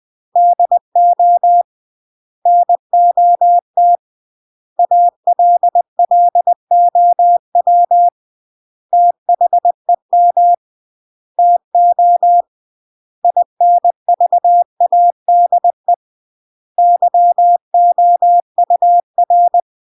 Morse code, heard as a keyed tone, about 20 wpm, 700 Hz. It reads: DO NOT ALLOW THEM TO INVADE YOUR
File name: CW / jodorowsky / 82 commandments 20-20